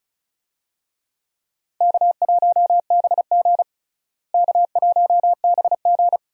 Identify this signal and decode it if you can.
Morse code, heard as a keyed tone, about 34 wpm, 700 Hz. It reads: K1BG K1BG
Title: USA Call Signs - 4 Characters - Repeated 2x - 35wpm